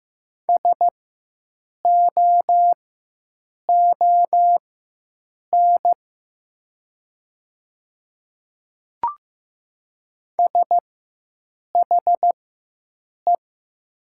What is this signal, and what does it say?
Morse code, heard as a keyed tone, about 15 wpm, spaced as for 7 wpm, 690 Hz. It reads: SOON SHE
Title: 4x Character Spacing - Top 500 Words - 15wpm